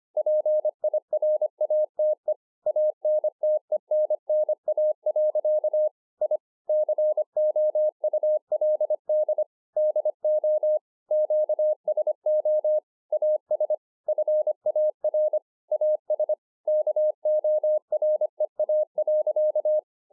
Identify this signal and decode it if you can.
Morse code, heard as a keyed tone, about 24 wpm, 610 Hz. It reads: PIRATE ANTENNA. I COULD DO QSO AS FAR AS KOREA.
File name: 2011-04-mp3 / BEGINNERS DIARY 25WPM